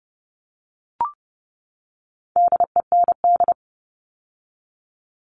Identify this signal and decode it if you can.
Morse code, heard as a keyed tone, about 30 wpm, 700 Hz. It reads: DEND
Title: All Cut Numbers - 4-Digit 30wpm